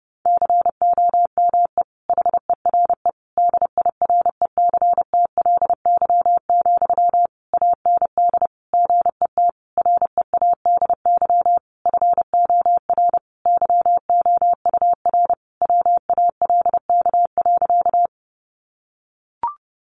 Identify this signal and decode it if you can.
Morse code, heard as a keyed tone, about 30 wpm, 700 Hz. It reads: COME HERE DIRECTLY, AND GET READY FOR YOUR WALK.
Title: Sentences from Top 800 Words 30wpm